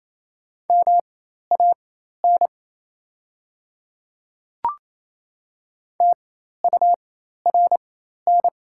Morse code, heard as a keyed tone, about 27 wpm, spaced as for 13 wpm, 700 Hz. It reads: MAN TURN
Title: 4x Character Spacing - Top 200 Words - 28wpm